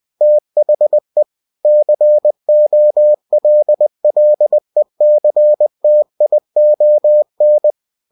Morse code, heard as a keyed tone, about 20 wpm, 600 Hz. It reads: THE COLLECTION
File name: cw-books / winnie-the-pooh / winnie-20wpm-page-0114